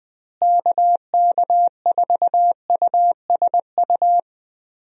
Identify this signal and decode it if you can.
Morse code, heard as a keyed tone, about 20 wpm, 700 Hz. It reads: KK4USU
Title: USA Call Signs - 6 Characters - Repeated 3x - 20wpm